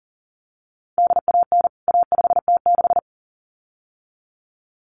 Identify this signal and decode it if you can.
Morse code, heard as a keyed tone, about 40 wpm, 700 Hz. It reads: DAN A5T6
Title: CWT Exchange 40wpm